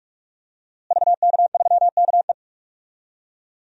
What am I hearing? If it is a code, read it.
Morse code, heard as a keyed tone, about 46 wpm, 700 Hz. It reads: VK3KE
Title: DXCC-100 Call Signs - 5 Characters - Repeated 3x - 45wpm